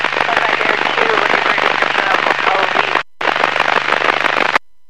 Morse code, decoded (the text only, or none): none